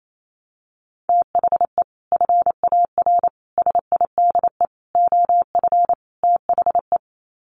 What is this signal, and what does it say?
Morse code, heard as a keyed tone, about 28 wpm, 700 Hz. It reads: THE FAR SIDE OF THE
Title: Sets of 5 Words 28wpm